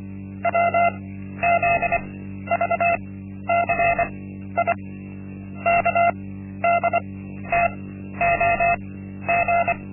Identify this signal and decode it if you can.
Morse code, heard as a keyed tone, about 24 wpm, spaced as for 12 wpm, 690 Hz. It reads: WZVCI KDTOG
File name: Linksmakalnis-1